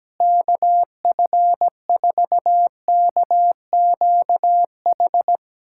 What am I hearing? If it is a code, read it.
Morse code, heard as a keyed tone, about 17 wpm, 700 Hz. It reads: KF4KQH
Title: USA Callsign DE Callsign - 6 Characters - 17wpm